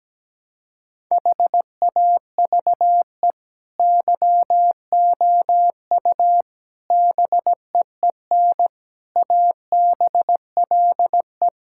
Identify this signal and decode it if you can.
Morse code, heard as a keyed tone, about 17 wpm, 700 Hz. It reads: HAVE YOU BEEN ABLE
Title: Sentences from Top 1000 Words - Review Part 6 - 17wpm